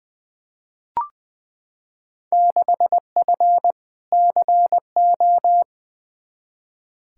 Morse code, heard as a keyed tone, about 20 wpm, 700 Hz. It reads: 6F CO
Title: Field Day Exchanges Encore - 20wpm